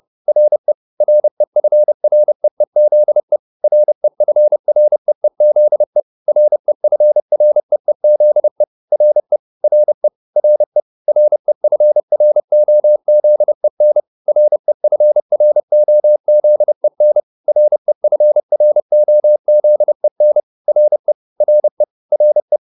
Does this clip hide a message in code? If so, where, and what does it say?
Yes: Morse code from 0.3 to 22.6 s, heard as a keyed tone, about 30 wpm, 600 Hz. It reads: RE REFREEZE REFREEZE REFREEZE RE RE RE REFROZEN REFROZEN REFROZEN RE RE RE